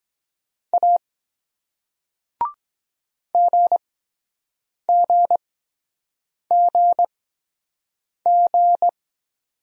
Morse code, heard as a keyed tone, about 23 wpm, spaced as for 7 wpm, 700 Hz. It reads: A GGGG